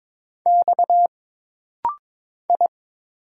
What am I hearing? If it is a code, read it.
Morse code, heard as a keyed tone, about 22 wpm, 700 Hz. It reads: X I
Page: This rapid-fire timing is geared to push your ICR (Instant Character Recognition) skills.